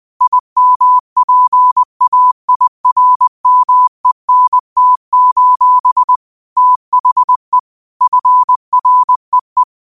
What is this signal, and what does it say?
Morse code, heard as a keyed tone, about 20 wpm, 1000 Hz. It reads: IMPAIRMENT: THE FREE
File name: qotdfiles / 20wpm